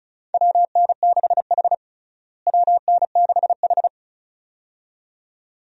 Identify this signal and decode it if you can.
Morse code, heard as a keyed tone, about 36 wpm, 700 Hz. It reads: WN6H WN6H
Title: USA Call Signs - 4 Characters - Repeated 2x - 35wpm